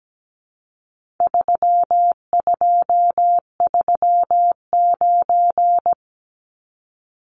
Morse code, heard as a keyed tone, about 17 wpm, 700 Hz. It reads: 3239
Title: Numbers - 4-Digit 17wpm